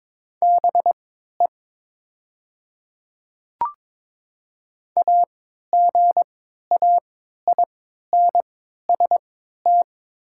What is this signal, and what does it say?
Morse code, heard as a keyed tone, about 22 wpm, spaced as for 13 wpm, 700 Hz. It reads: BE AGAINST
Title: 3x Character Spacing - Top 500 Words - 22wpm